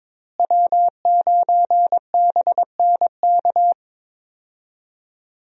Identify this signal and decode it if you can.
Morse code, heard as a keyed tone, about 22 wpm, 700 Hz. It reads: W9BNK
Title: USA Call Signs - 5 Characters - Repeated 2x - 22wpm